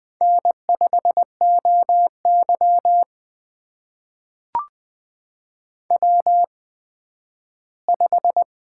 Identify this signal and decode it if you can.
Morse code, heard as a keyed tone, about 20 wpm, spaced as for 12 wpm, 700 Hz. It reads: N5OY W5
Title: USA Call Signs - 4 Characters - 8x Character Spacing - 20wpm